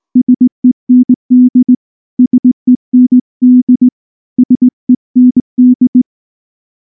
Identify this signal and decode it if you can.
Morse code, heard as a keyed tone, about 18 wpm, 250 Hz. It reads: SEND SEND SEND
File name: morse / morse-training-20WPM-9